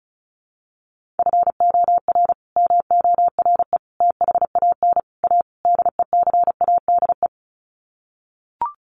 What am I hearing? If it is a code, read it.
Morse code, heard as a keyed tone, about 34 wpm, 700 Hz. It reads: FOR MORE THAN A DECADE
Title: Sets of 5 Words 35wpm